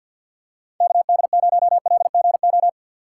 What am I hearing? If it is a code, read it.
Morse code, heard as a keyed tone, about 49 wpm, 700 Hz. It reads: KD0LGO
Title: USA Call Signs - 6 Characters - 3x Character Spacing - 50wpm